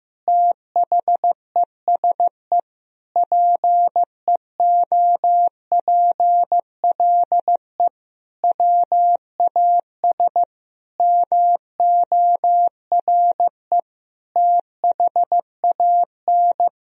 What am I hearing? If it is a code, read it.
Morse code, heard as a keyed tone, about 15 wpm, 700 Hz. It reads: THESE PEOPLE WAS MORE THAN
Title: Sentences from Top 1000 Words - Review Part 2 - 15wpm